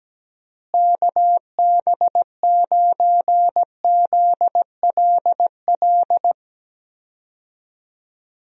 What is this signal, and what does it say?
Morse code, heard as a keyed tone, about 17 wpm, 700 Hz. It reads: KB9ZLL
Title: USA Call Signs - 6 Characters - Repeated 3x - 17wpm